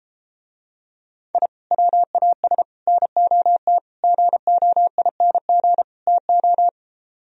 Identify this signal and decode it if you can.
Morse code, heard as a keyed tone, about 34 wpm, 700 Hz. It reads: I WAS NOT GOING TO